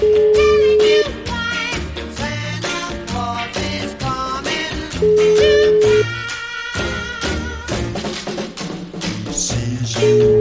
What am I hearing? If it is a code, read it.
Ringback tone (Europe, 425 Hz), with noise.